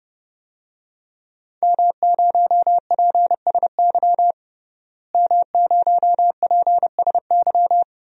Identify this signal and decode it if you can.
Morse code, heard as a keyed tone, about 30 wpm, 700 Hz. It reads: M0PSY M0PSY
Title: DXCC-100 Call Signs - 5 Characters - Repeated 2x - 30wpm